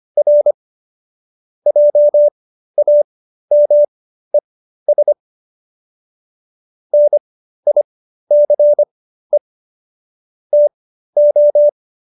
Morse code, heard as a keyed tone, about 25 wpm, spaced as for 13 wpm, 590 Hz. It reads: R JAMES NICE TO